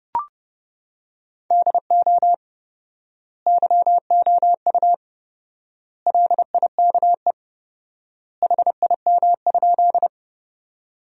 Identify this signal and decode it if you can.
Morse code, heard as a keyed tone, about 32 wpm, 700 Hz. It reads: DO YOU LIKE HIM?
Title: 4x Word-Spacing - Sentences from Top 100 Words - 30wpm